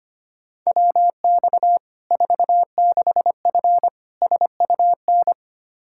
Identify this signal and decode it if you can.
Morse code, heard as a keyed tone, about 25 wpm, 700 Hz. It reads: WX 46F SUN
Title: Practice QSO - Exchange 2 - Info - 25wpm